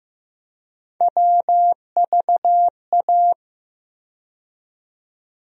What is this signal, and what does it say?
Morse code, heard as a keyed tone, about 15 wpm, 700 Hz. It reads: WVA
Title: USA Call Sign Postfixes - 3 Letters 15wpm